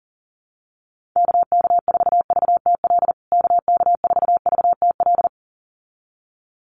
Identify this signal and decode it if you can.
Morse code, heard as a keyed tone, about 40 wpm, 700 Hz. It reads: KK4VTL KK4VTL
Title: USA Call Signs Repeated 2x 40wpm